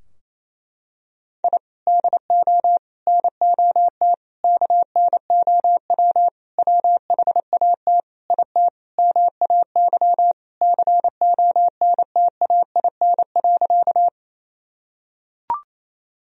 Morse code, heard as a keyed tone, about 28 wpm, 700 Hz. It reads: I DO NOT KNOW WHAT IT MAY CONTAIN.
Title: Sentences from Top 1000 Words - Review Part 2 - 28wpm